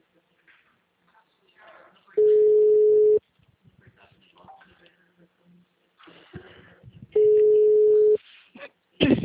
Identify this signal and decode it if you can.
Ringback tone (Europe, 425 Hz).
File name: out- -2005- - - .675